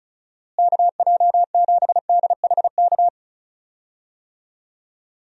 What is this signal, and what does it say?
Morse code, heard as a keyed tone, about 35 wpm, 700 Hz. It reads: KJ7DHK
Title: USA Call Signs - 6 Characters - Repeated 2x - 35wpm